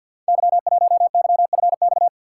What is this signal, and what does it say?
Morse code, heard as a keyed tone, about 46 wpm, 700 Hz. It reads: Y1YFX
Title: Code Groups - 5 Letter:Numbers - 50wpm